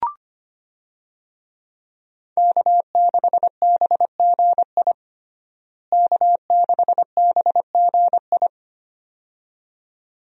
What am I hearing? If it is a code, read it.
Morse code, heard as a keyed tone, about 25 wpm, 700 Hz. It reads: K6BGI K6BGI
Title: USA Call Signs - 5 Characters - Repeated 2x - 25wpm